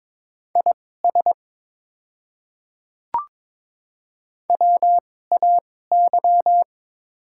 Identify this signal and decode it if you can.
Morse code, heard as a keyed tone, about 22 wpm, spaced as for 16 wpm, 700 Hz. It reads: IS WAY
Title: 2x Character Spacing - Top 100 Words - 22wpm